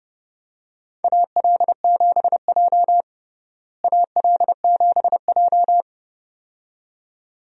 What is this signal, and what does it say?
Morse code, heard as a keyed tone, about 32 wpm, 700 Hz. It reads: AL7J AL7J
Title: USA Call Signs - 4 Characters - Repeated 2x - 30wpm